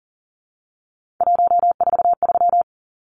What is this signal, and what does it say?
Morse code, heard as a keyed tone, about 40 wpm, 700 Hz. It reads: 143
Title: Numbers - 3-Digit 40wpm